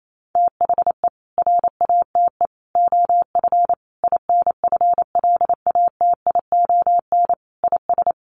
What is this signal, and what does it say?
Morse code, heard as a keyed tone, about 28 wpm, 700 Hz. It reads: THE RATE OF INFLATION IS
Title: Sentences from Top 1100 Words 28wpm